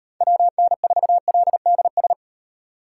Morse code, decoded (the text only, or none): WN4LDS